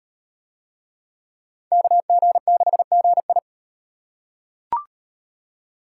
Morse code, heard as a keyed tone, about 38 wpm, 700 Hz. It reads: KG6GI